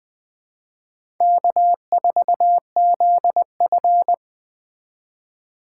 Morse code, heard as a keyed tone, about 20 wpm, 700 Hz. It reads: K4ZF
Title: USA Call Signs - 4 Characters - 2x Character Spacing - 20wpm